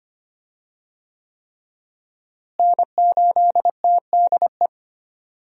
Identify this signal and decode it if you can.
Morse code, heard as a keyed tone, about 25 wpm, 700 Hz. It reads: N8TDE
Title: USA Call Signs - 5 Characters - Repeated 3x - 25wpm